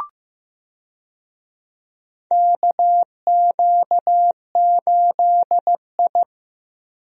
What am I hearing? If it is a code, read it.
Morse code, heard as a keyed tone, about 15 wpm, 700 Hz. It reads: KQ8I
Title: USA Call Signs - 4 Characters - Repeated 3x - 15wpm